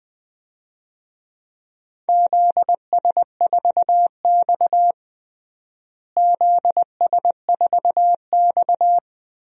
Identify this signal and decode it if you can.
Morse code, heard as a keyed tone, about 21 wpm, 700 Hz. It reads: ZS4X ZS4X
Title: DXCC-100 Call Signs - 4 Characters - Repeated 3x - 20wpm